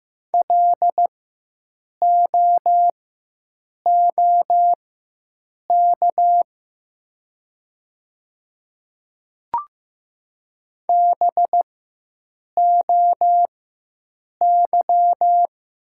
Morse code, heard as a keyed tone, about 15 wpm, spaced as for 7 wpm, 700 Hz. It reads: LOOK BOY
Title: 4x Character Spacing - Top 200 Words - 15wpm